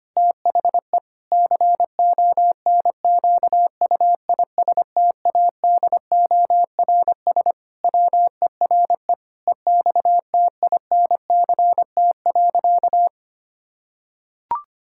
Morse code, heard as a keyed tone, about 25 wpm, 700 Hz. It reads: THE CONQUISTADORS WERE EXTINCT.
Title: Pangram Challenge v2 - 25wpm